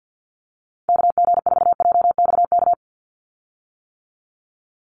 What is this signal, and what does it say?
Morse code, heard as a keyed tone, about 49 wpm, 700 Hz. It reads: KG4JXK
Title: USA Call Signs 50wpm